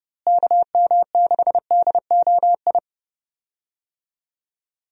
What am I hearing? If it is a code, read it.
Morse code, heard as a keyed tone, about 30 wpm, 700 Hz. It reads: KM6DOI
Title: USA Call Signs - 6 Characters - Repeated 3x - 30wpm